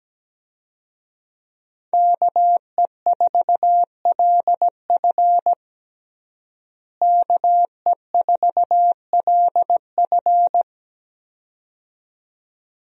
Morse code, heard as a keyed tone, about 17 wpm, 700 Hz. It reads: KE4LF KE4LF
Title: USA Call Signs - 5 Characters - Repeated 2x - 17wpm